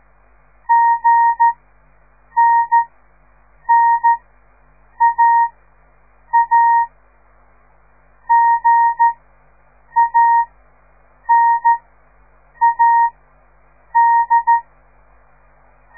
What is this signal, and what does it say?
Morse code, heard as a keyed tone, about 13 wpm, spaced as for 7 wpm, 930 Hz. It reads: GNNAA GANAD
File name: radiomorningstar / morse / les 4